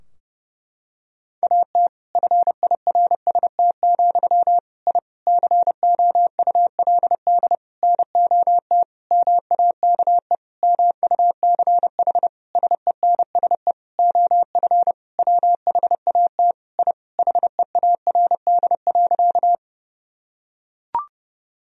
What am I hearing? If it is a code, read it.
Morse code, heard as a keyed tone, about 30 wpm, 700 Hz. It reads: AT FIRST, I COULD NOT MAKE MUCH SENSE OF WHAT I HEARD.
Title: Sentences from Top 1000 Words - Review Part 4 - 30wpm